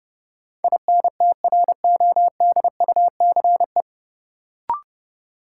Morse code, heard as a keyed tone, about 30 wpm, 700 Hz. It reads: INTRODUCE
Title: Sentences from Top 1600 Words - New Word Drill - 30wpm